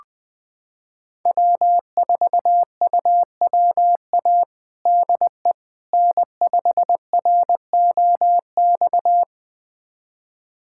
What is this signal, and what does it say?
Morse code, heard as a keyed tone, about 21 wpm, 700 Hz. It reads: W4UWA DE N5ROX
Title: USA Callsign DE Callsign - 5 Characters - 20wpm